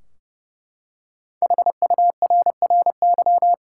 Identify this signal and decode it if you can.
Morse code, heard as a keyed tone, about 30 wpm, 700 Hz. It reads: HURRY
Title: Five Letter Words - 30wpm